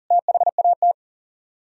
Morse code, heard as a keyed tone, about 53 wpm, spaced as for 47 wpm, 700 Hz. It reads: THAT
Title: Speed-Racing - Sets of 5 Words 40wpm